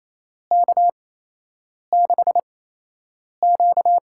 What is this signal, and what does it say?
Morse code, heard as a keyed tone, about 28 wpm, spaced as for 8 wpm, 700 Hz. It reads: K6Q
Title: USA Call Signs - 4 Characters - 8x Character Spacing - 28wpm